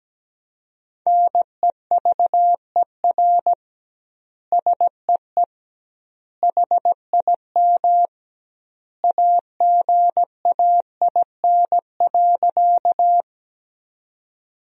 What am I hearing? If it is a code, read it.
Morse code, heard as a keyed tone, about 17 wpm, 700 Hz. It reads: NEVER SEE HIM AGAIN.
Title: 2x Word-Spacing - Sentences from Top 300 Words - 17wpm